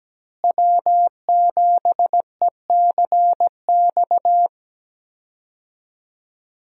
Morse code, heard as a keyed tone, about 17 wpm, 700 Hz. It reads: W7ECX
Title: USA Call Signs - 5 Characters - Repeated 2x - 17wpm